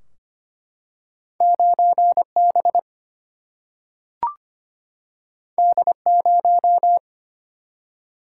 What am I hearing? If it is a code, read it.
Morse code, heard as a keyed tone, about 25 wpm, 700 Hz. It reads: 9 B D 0
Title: Single-Letter Single-Number 25wpm